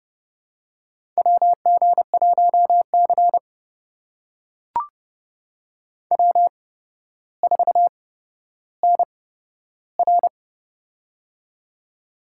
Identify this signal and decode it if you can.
Morse code, heard as a keyed tone, about 30 wpm, spaced as for 13 wpm, 700 Hz. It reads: WG1C W4NR
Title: USA Call Signs - 4 Characters - 8x Character Spacing - 30wpm